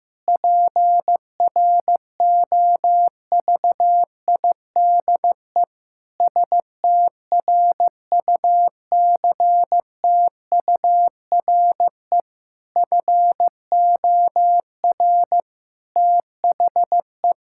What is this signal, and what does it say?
Morse code, heard as a keyed tone, about 15 wpm, 700 Hz. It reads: PROVIDE STRUCTURE FOR THE
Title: Sentences from Top 1000 Words - Review Part 6 - 15wpm